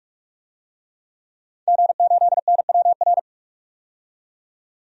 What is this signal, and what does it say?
Morse code, heard as a keyed tone, about 46 wpm, 700 Hz. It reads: G8NWR
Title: DXCC-100 Call Signs - 5 Characters - 45wpm